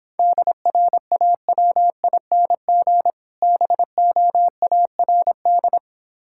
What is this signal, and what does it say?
Morse code, heard as a keyed tone, about 26 wpm, 700 Hz. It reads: DRAWING BOARD